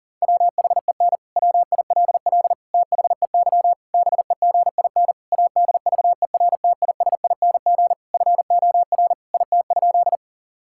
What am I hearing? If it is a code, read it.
Morse code, heard as a keyed tone, about 40 wpm, 700 Hz. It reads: WHEN WILL THEY BEGIN ADVERTISING FOR IT?
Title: Sentences from Top 1000 Words - Review Part 6 - 40wpm